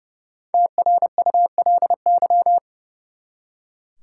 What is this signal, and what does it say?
Morse code, heard as a keyed tone, about 30 wpm, 700 Hz. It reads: TRULY
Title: Five Letter Words - 30wpm